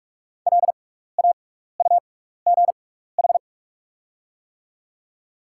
Morse code, heard as a keyed tone, about 44 wpm, spaced as for 15 wpm, 700 Hz. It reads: LAUGH
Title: 6x Character Spacing - Top 500 Words - 45wpm